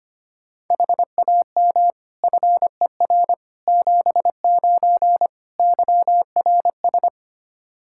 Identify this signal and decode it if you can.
Morse code, heard as a keyed tone, about 25 wpm, 700 Hz. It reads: HAM FER 79 YRS
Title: Practice QSO - Exchange 3 - More Info - 25wpm